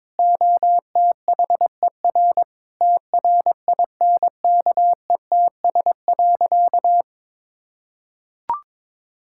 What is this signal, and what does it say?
Morse code, heard as a keyed tone, about 22 wpm, 700 Hz. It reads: OTHER TRINKETS.
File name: practice / 22wpm / Sentences with 2-Word Phrases - 22wpm